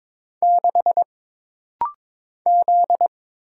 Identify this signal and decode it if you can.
Morse code, heard as a keyed tone, about 22 wpm, 700 Hz. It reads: 6 Z
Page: This rapid-fire timing is geared to push your ICR (Instant Character Recognition) skills.